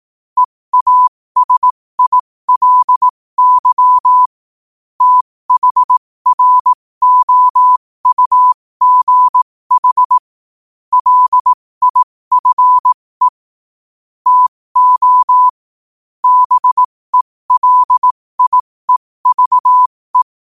Morse code, heard as a keyed tone, about 18 wpm, spaced as for 15 wpm, 1000 Hz. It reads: EASILY THROUGH LIFE TO BELIEVE